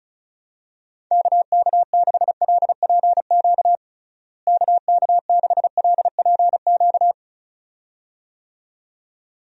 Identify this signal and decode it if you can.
Morse code, heard as a keyed tone, about 36 wpm, 700 Hz. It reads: KK6LPQ KK6LPQ
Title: USA Call Signs - 6 Characters - Repeated 2x - 35wpm